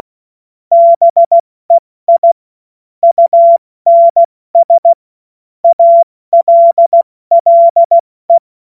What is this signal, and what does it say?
Morse code, heard as a keyed tone, about 16 wpm, 680 Hz. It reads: BEI UNS ALLE
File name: Morsekurs / Witze / Witz 25. Oktober 2022 70 F80